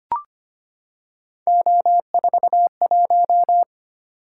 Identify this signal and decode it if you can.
Morse code, heard as a keyed tone, about 25 wpm, 700 Hz. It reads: O41
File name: Three Letter-Number Combinations - 25wpm